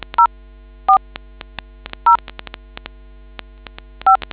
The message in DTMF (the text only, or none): 04 0 5